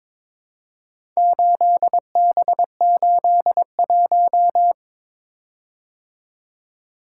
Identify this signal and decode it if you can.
Morse code, heard as a keyed tone, about 22 wpm, 700 Hz. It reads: 8B81
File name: practice / 22wpm / Four Letter-Number Combinations - 22wpm